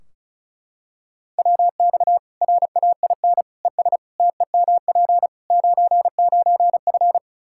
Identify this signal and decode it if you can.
Morse code, heard as a keyed tone, about 36 wpm, 700 Hz. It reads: WX RAIN ES TEMP 99F
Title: Practice QSO - Exchange 2 - Info - 35wpm